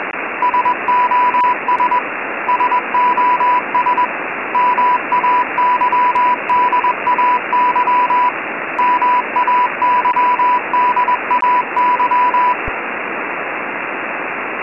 Morse code, heard as a keyed tone, about 21 wpm, 990 Hz, with radio noise. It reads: SOS SOS MAYDAY MAYDAY
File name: Sos Mayday 988 Hz Freq 20wpm W Heavy Background Static